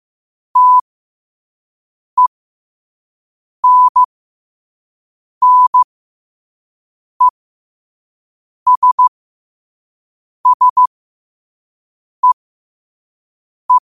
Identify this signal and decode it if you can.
Morse code, heard as a keyed tone, about 15 wpm, spaced as for 5 wpm, 1000 Hz. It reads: TENNESSEE
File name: qotdfiles / 5wpm